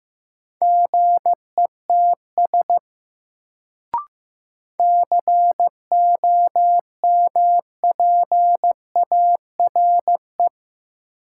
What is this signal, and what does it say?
Morse code, heard as a keyed tone, about 15 wpm, 700 Hz. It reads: GETS COMPARE
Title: Sentences from Top 1300 Words - New Word Drill - 15wpm